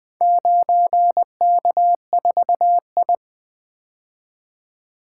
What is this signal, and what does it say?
Morse code, heard as a keyed tone, about 20 wpm, 700 Hz. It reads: 9K4I
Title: Four Letter-Number Combinations - 20wpm